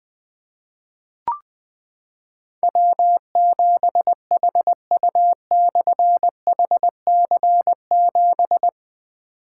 Morse code, heard as a keyed tone, about 20 wpm, 700 Hz. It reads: W7HU/HC7
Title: Difficult Call Signs 20wpm